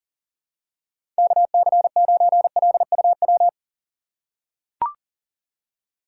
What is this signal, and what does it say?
Morse code, heard as a keyed tone, about 40 wpm, 700 Hz. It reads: KC9LUW
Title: USA Call Signs - 6 Characters - 6x Character Spacing - 40wpm